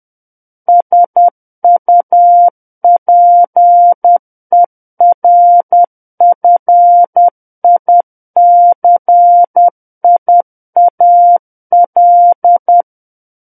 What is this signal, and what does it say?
Morse code, heard as a keyed tone, about 10 wpm, 700 Hz. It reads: SUPERFICIAL
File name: Book One / 01-Chapter 1 - 10 WPM